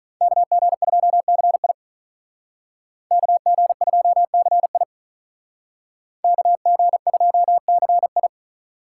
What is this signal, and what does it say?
Morse code, heard as a keyed tone, about 43 wpm, 700 Hz. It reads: KG2CI KG2CI KG2CI
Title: Speed Racing - USA Call Signs 35wpm